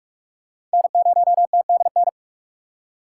Morse code, heard as a keyed tone, about 42 wpm, 700 Hz. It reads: N0TDN
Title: USA Call Signs - 5 Characters - 3x Character Spacing - 45wpm